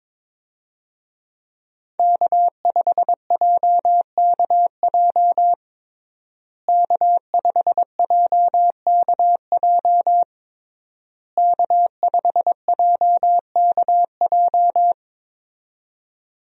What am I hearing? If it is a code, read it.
Morse code, heard as a keyed tone, about 22 wpm, 700 Hz. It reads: K5JKJ K5JKJ K5JKJ
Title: USA Call Signs - 5 Characters - Repeated 3x - 22wpm